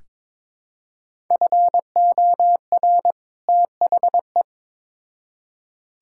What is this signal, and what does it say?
Morse code, heard as a keyed tone, about 22 wpm, 700 Hz. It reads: FOR THE
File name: practice / 22wpm / Build A Phrase - 2-Word Phrases - 22wpm